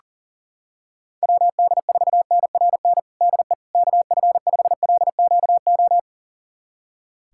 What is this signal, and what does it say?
Morse code, heard as a keyed tone, about 37 wpm, 700 Hz. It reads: WD4NRN DE KF5LQO
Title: USA Callsign DE Callsign - 6 Characters - 40wpm